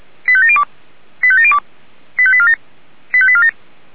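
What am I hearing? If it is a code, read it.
Five-tone selective call (ZVEI), heard as quick runs of five tones, tones of about 70 ms. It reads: 85702 85702 85847 85847
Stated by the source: ZVEI five-tone call 89200 received by SDR.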